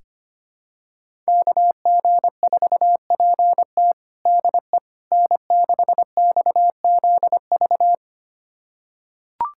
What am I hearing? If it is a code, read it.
Morse code, heard as a keyed tone, about 25 wpm, 700 Hz. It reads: KG4PT DE N6XZV
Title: USA Callsign DE Callsign - 5 Characters - 25wpm